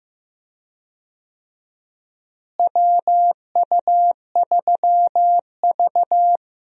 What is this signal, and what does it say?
Morse code, heard as a keyed tone, about 15 wpm, 700 Hz. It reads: WU3V
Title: USA Call Signs - 4 Characters - Repeated 3x - 15wpm